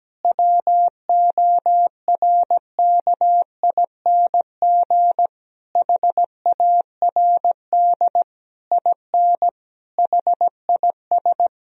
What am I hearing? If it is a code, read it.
Morse code, heard as a keyed tone, about 17 wpm, 700 Hz. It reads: WORKING HARD IN HIS
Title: Sentences from Top 1000 Words - Review Part 4 - 17wpm